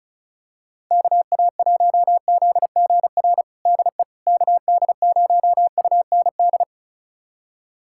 Morse code, heard as a keyed tone, about 34 wpm, 700 Hz. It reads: KA1ZGR DE KD0UND
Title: USA Callsign DE Callsign - 6 Characters - 35wpm